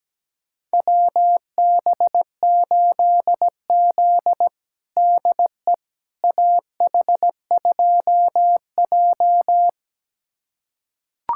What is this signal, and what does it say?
Morse code, heard as a keyed tone, about 17 wpm, 700 Hz. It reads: WB8Z DE AH2J
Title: USA Callsign DE Callsign - 4 Characters - 17wpm